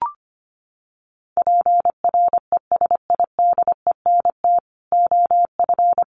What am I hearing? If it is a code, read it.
Morse code, heard as a keyed tone, about 25 wpm, 700 Hz. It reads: PRESIDENT OF
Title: Sets of 4 Words 25wpm